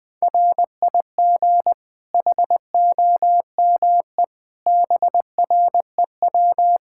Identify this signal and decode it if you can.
Morse code, heard as a keyed tone, about 20 wpm, 700 Hz. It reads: RIG HOME BREW
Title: K1USN SST QSO Practice 20wpm